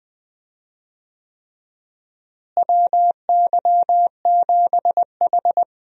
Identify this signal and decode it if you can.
Morse code, heard as a keyed tone, about 20 wpm, 700 Hz. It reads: WY7H
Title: USA Call Signs - 4 Characters - Repeated 3x - 20wpm